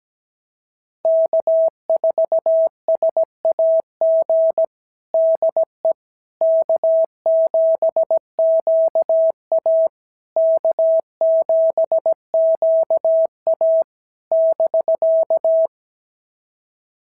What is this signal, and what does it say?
Morse code, heard as a keyed tone, about 17 wpm, 650 Hz. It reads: K4SAG DE K7QA K7QA <BK>
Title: SOTA QSO Practice 17wpm